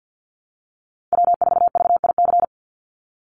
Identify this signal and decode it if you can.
Morse code, heard as a keyed tone, about 50 wpm, 700 Hz. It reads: W4VIC
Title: Call Signs Repeated 2x - 50wpm